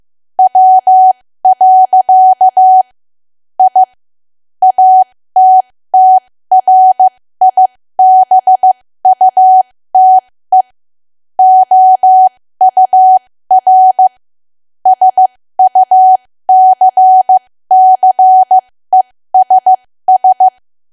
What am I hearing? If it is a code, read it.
Morse code, heard as a keyed tone, about 15 wpm, spaced as for 13 wpm, 750 Hz. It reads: W. I ATTRIBUTE OUR SUCCESS
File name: cwa / W1AW-Level2 / 13WPM